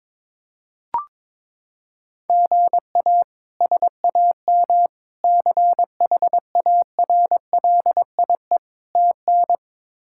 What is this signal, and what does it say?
Morse code, heard as a keyed tone, about 22 wpm, 700 Hz. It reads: GA SAM CHARLIE TN
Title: K1USN SST QSO Practice - 22wpm